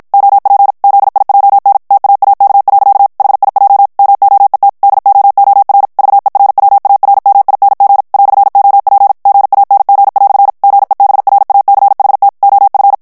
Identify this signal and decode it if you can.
Morse code, heard as a keyed tone, about 56 wpm, 800 Hz. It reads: OP ZIJN TAAK. HIJ MOET DOOR VERWARMING "OP GANG" GEBRACHT OF
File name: morse mp3 / HOLLANDAIS texte clair 2 vitesse 50 mots minute